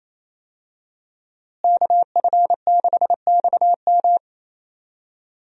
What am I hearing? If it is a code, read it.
Morse code, heard as a keyed tone, about 28 wpm, 700 Hz. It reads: KF6XM
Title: USA Call Signs - 5 Characters - Repeated 3x - 28wpm